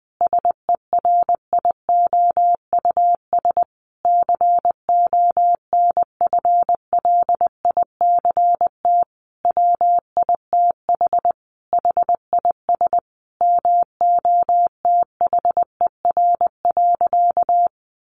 Morse code, heard as a keyed tone, about 19 wpm, 700 Hz. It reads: SERIOUS CONFLICT WITH HIS MOTHER.
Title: Sentences from Top 1300 Words 20wpm